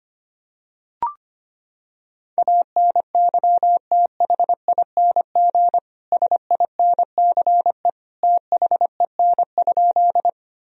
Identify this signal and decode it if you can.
Morse code, heard as a keyed tone, about 25 wpm, 700 Hz. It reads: ANYTHING SINCE THEN?
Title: Sentences from Top 1000 Words - Review Part 1 - 25wpm